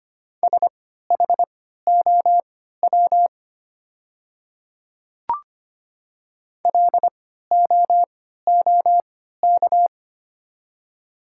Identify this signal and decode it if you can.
Morse code, heard as a keyed tone, about 25 wpm, spaced as for 14 wpm, 700 Hz. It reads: SHOW LOOK
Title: 3x Character Spacing - Top 300 Words - 25wpm